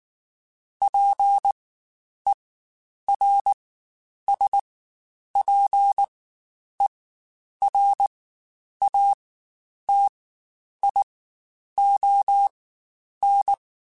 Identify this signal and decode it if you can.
Morse code, heard as a keyed tone, about 19 wpm, spaced as for 9 wpm, 800 Hz. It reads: PERSPERATION
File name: Quotes Spaced / Quotes Spaced-mp3s / Persperation sp @18